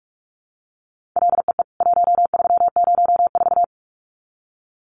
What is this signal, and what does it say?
Morse code, heard as a keyed tone, about 44 wpm, 700 Hz. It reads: LEE 1304
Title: CWT Exchange 45wpm